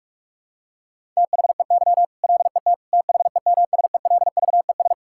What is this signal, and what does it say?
Morse code, heard as a keyed tone, about 46 wpm, 700 Hz. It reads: THEY LET THEMSELVES
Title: Sentences from Top 1000 Words - Review Part 5 - 45wpm